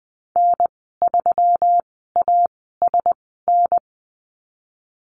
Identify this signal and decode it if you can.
Morse code, heard as a keyed tone, about 19 wpm, spaced as for 14 wpm, 700 Hz. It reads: N3ASN
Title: 207 Lesson 33g - Letter j - Call Signs 20wpm